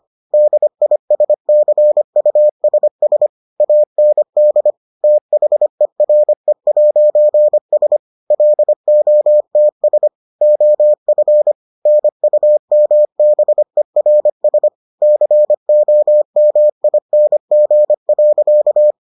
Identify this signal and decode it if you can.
Morse code, heard as a keyed tone, about 25 wpm, 600 Hz. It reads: DISCUSS AND THERE'S LOTS OF NUMBERS COMING.